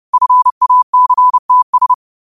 Morse code, heard as a keyed tone, about 30 wpm, 1000 Hz. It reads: FACTS
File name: qotdfiles / 30wpm